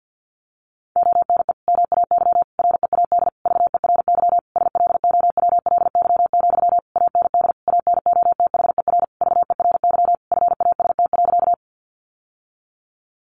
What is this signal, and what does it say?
Morse code, heard as a keyed tone, about 50 wpm, 700 Hz. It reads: ONE MAY READ VERY SLOWLY, AND ANOTHER VERY FAST.
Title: Sentences from Top 1100 Words 50wpm